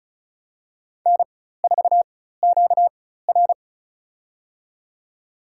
Morse code, heard as a keyed tone, about 36 wpm, spaced as for 16 wpm, 700 Hz. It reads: N4QR
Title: USA Call Signs - 4 Characters - 4x Character Spacing - 35wpm